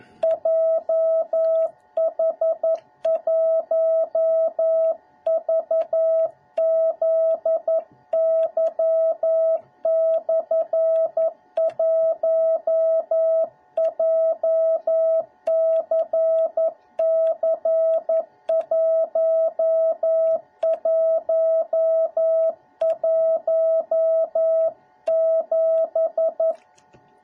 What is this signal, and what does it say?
Morse code, heard as a keyed tone, about 11 wpm, 650 Hz. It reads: JH1VZY/1 JCC1117